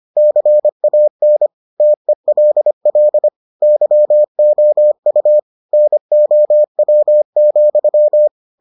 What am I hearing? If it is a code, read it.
Morse code, heard as a keyed tone, about 25 wpm, 600 Hz. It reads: CAN TELL YOU NOW,